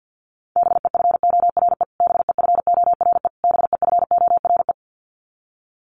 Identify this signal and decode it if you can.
Morse code, heard as a keyed tone, about 49 wpm, 700 Hz. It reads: BEFORE BEFORE BEFORE
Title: Top 300 Words Repeated 50wpm